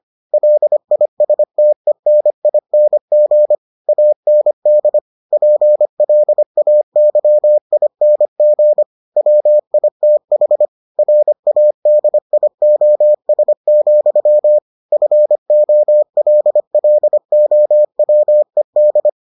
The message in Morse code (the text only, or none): LISTENING AND PLAYING WITH RADIOS, FOLLOWED